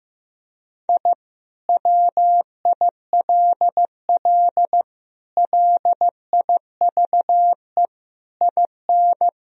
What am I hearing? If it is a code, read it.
Morse code, heard as a keyed tone, about 15 wpm, 700 Hz. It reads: I WILL LIVE IN
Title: Sentences from Top 1000 Words - Review Part 2 - 15wpm